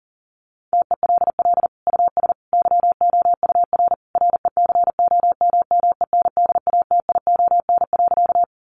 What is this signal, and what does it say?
Morse code, heard as a keyed tone, about 40 wpm, 700 Hz. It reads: TELL US YOUR RECOMMENDATION.
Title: Sentences from Top 1400 Words 40wpm